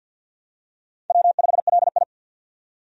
Morse code, heard as a keyed tone, about 51 wpm, 700 Hz. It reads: W5LI
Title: USA Call Signs - 4 Characters - 50wpm